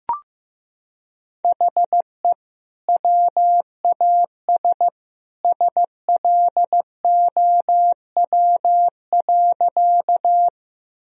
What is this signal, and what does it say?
Morse code, heard as a keyed tone, about 15 wpm, 700 Hz. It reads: HE WAS SLOW.
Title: Sentences from Top 1000 Words - Review Part 2 - 15wpm